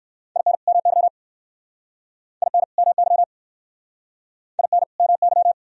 Morse code, heard as a keyed tone, about 59 wpm, 700 Hz. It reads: IN MY IN MY IN MY